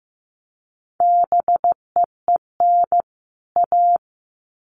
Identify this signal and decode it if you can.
Morse code, heard as a keyed tone, about 15 wpm, 700 Hz. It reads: BEEN A
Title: Sets of 2 Words 15wpm